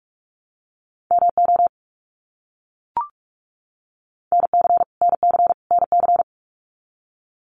Morse code, heard as a keyed tone, about 44 wpm, 700 Hz. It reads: MO NC NC NC
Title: US States Repeated 3x 45wpm